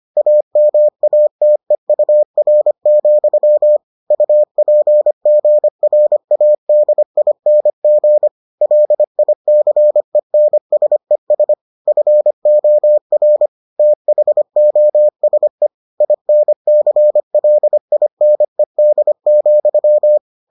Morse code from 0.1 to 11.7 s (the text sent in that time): AMATEUR, UPGRADING LICENSES